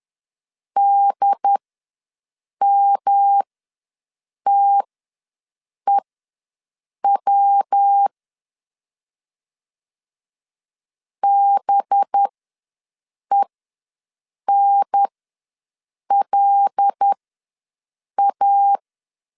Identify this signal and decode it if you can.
Morse code, heard as a keyed tone, about 10 wpm, spaced as for 6 wpm, 800 Hz. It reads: DMTEW BENLA
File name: LEZIONI / 5 LBVC / 5 LBV ws 10wpm 3645